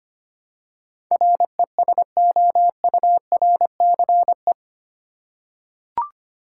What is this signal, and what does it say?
Morse code, heard as a keyed tone, about 25 wpm, 700 Hz. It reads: RESOURCE
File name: Eight Letter Words - 25wpm